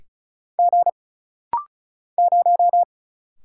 Morse code, heard as a keyed tone, about 36 wpm, 700 Hz. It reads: G 0
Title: Single Letter-Number - Rapid-Fire 35wpm